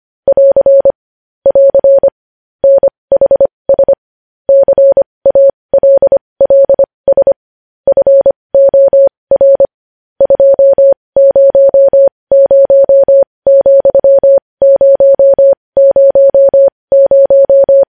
Morse code, heard as a keyed tone, about 25 wpm, spaced as for 20 wpm, 550 Hz. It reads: <AR> <AR> NHS CALLS FOR 200,000
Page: News Headlines in Morse Code at 20 WPM